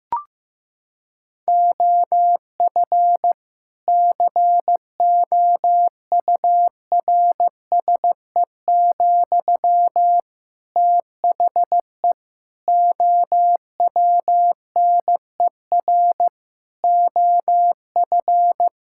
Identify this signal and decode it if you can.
Morse code, heard as a keyed tone, about 15 wpm, 700 Hz. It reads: OF COURSE, THE OWNER OF
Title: Sentences from Top 1600 Words 15wpm